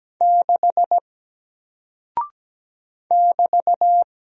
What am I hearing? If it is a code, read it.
Morse code, heard as a keyed tone, about 17 wpm, 700 Hz. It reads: 6 <BT>
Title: Single Character (Letter:Number:Punctuation:Prosign) Alt 17wpm